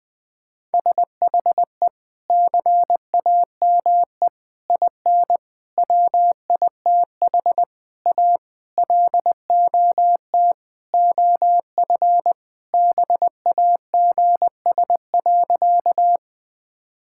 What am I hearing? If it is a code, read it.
Morse code, heard as a keyed tone, about 20 wpm, 700 Hz. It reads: SHE CAME IN WITH A LOT OF BAGS.
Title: Sentences with 4-Word Phrases - 20wpm